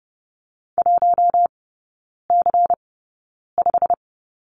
Morse code, heard as a keyed tone, about 30 wpm, spaced as for 9 wpm, 690 Hz. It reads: 1C5
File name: Lesson 16e - Letter c - Three Characters 30wpm